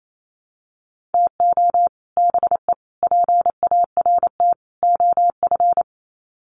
Morse code, heard as a keyed tone, about 28 wpm, 700 Hz. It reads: TO BE PART OF
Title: Sets of 4 Words 28wpm